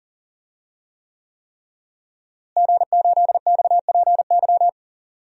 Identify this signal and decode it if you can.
Morse code, heard as a keyed tone, about 40 wpm, 700 Hz. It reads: G8XPY
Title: DXCC-100 Call Signs - 5 Characters - Repeated 3x - 40wpm